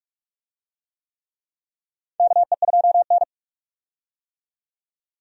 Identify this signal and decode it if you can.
Morse code, heard as a keyed tone, about 46 wpm, 700 Hz. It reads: KE2N
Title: USA Call Signs - 4 Characters - 45wpm